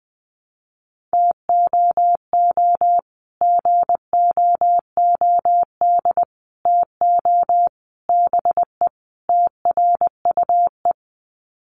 Morse code, heard as a keyed tone, about 20 wpm, 700 Hz. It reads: TOO GOOD TO BE TRUE
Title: Sets of 5 Words Encore 20wpm